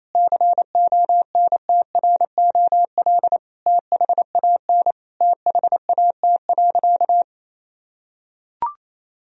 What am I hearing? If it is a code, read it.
Morse code, heard as a keyed tone, about 28 wpm, 700 Hz. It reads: CONTROL THAN THAT.
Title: Sentences from Top 1000 Words - Review Part 3 - 28wpm